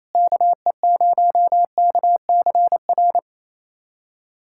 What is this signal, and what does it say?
Morse code, heard as a keyed tone, about 27 wpm, 700 Hz. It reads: KE0KCR
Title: USA Callsign DE Callsign - 6 Characters - 28wpm